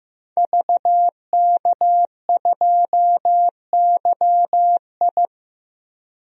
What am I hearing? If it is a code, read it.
Morse code, heard as a keyed tone, about 15 wpm, 700 Hz. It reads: VK2YI
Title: DXCC-100 Call Signs - 5 Characters - Repeated 2x - 15wpm